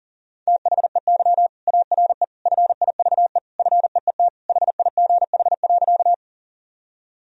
Morse code, heard as a keyed tone, about 40 wpm, 700 Hz. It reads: THEY ARE FIVE FEET HIGH.
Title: Sentences from Top 1000 Words - Review Part 2 - 40wpm